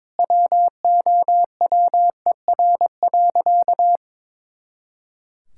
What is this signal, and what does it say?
Morse code, heard as a keyed tone, about 22 wpm, 700 Hz. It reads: WOWER.
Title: Pangram Challenge v2 - 22wpm